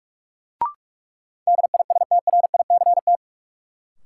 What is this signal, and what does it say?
Morse code, heard as a keyed tone, about 46 wpm, 700 Hz. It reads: DISTRICT
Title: Sentences from Top 1600 Words - New Word Drill Alt - 45wpm